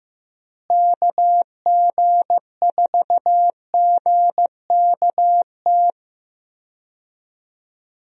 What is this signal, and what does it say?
Morse code, heard as a keyed tone, about 16 wpm, 700 Hz. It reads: KG4GKT
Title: USA Call Signs - 6 Characters - Repeated 2x - 15wpm